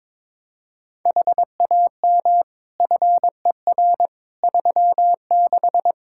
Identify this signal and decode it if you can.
Morse code, heard as a keyed tone, about 22 wpm, 700 Hz. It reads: HAM FER 36
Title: Practice QSO - Exchange 3 - More Info - 22wpm